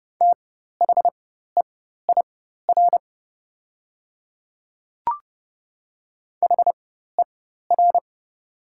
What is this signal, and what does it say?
Morse code, heard as a keyed tone, about 31 wpm, spaced as for 14 wpm, 700 Hz. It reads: THEIR HER
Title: 4x Character Spacing - Top 300 Words - 30wpm